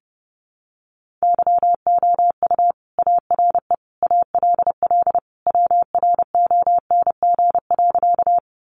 Morse code, heard as a keyed tone, about 30 wpm, 700 Hz. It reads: YOU ARE ALL WRONG.
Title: Sentences from Top 900 Words 30wpm